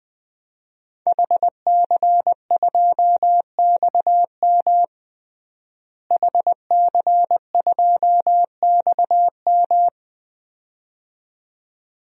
Morse code, heard as a keyed tone, about 20 wpm, 700 Hz. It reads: HC2XM HC2XM
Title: DXCC-100 Call Signs - 5 Characters - Repeated 2x - 20wpm